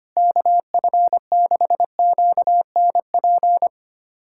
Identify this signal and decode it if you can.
Morse code, heard as a keyed tone, about 25 wpm, 700 Hz. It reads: KF6QNP
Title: USA Callsign DE Callsign - 6 Characters - 25wpm